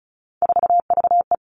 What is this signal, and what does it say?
Morse code, heard as a keyed tone, about 36 wpm, 700 Hz. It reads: 4VE
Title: Three Challenging Letter-Number Combinations 35wpm